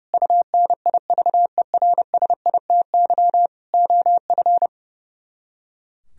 Morse code, heard as a keyed tone, about 34 wpm, 700 Hz. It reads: UNIVERSITY OF
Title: Speed-Racing - Sets of 4 Words 30wpm